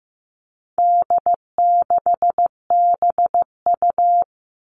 Morse code, heard as a keyed tone, about 15 wpm, 700 Hz. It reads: D6BU
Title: All Cut Numbers - 4-Digit 15wpm